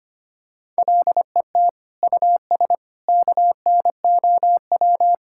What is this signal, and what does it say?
Morse code, heard as a keyed tone, about 25 wpm, 700 Hz. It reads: LET US KNOW
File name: practice / 25wpm / Sentences with 9-Word Phrases - 25wpm